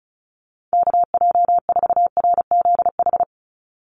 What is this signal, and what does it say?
Morse code, heard as a keyed tone, about 34 wpm, 700 Hz. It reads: KJ4RZH
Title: USA Call Signs 35wpm